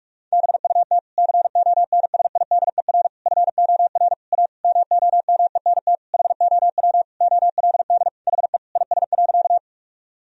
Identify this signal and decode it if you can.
Morse code, heard as a keyed tone, about 46 wpm, 700 Hz. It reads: BUT CONSIDER FOR A MOMENT HOW OLD HE IS.
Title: Sentences from Top 1000 Words - Review Part 4 - 45wpm